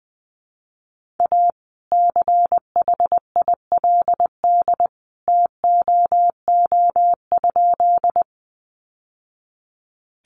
Morse code, heard as a keyed tone, about 20 wpm, 700 Hz. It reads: A CHILD TOO?
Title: Sentences from Top 800 Words 20wpm